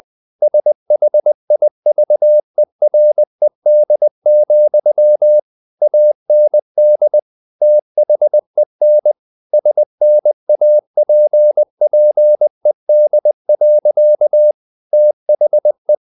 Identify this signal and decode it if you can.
Morse code, heard as a keyed tone, about 20 wpm, 600 Hz. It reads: SHIVERED, AND THEN SNAPPED. THE
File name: the old man and the sea part 2 20wpm